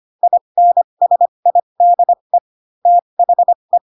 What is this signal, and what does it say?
Morse code, heard as a keyed tone, about 25 wpm, spaced as for 22 wpm, 700 Hz. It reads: INSIDE THE